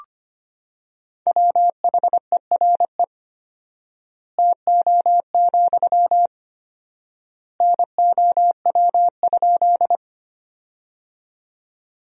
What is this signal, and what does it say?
Morse code, heard as a keyed tone, about 25 wpm, 700 Hz. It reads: WHERE TO, NOW?
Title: 4x Word-Spacing - Sentences from Top 200 Words - 25wpm